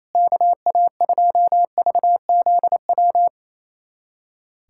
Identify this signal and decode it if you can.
Morse code, heard as a keyed tone, about 28 wpm, 700 Hz. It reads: KA2VZW
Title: USA Callsign DE Callsign 28wpm